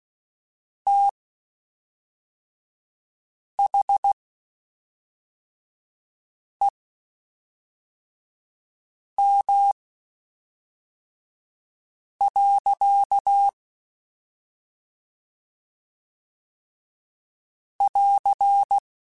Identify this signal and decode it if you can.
Morse code, heard as a keyed tone, about 16 wpm, spaced as for 3 wpm, 800 Hz. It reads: THEM. <AR>
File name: Spaced / Chuckles Spaced-mp3s / SoftWords sp @5